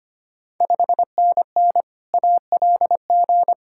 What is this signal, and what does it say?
Morse code, heard as a keyed tone, about 25 wpm, 700 Hz. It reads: 5NN ALG
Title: Ontario QSO Party Exchange 25wpm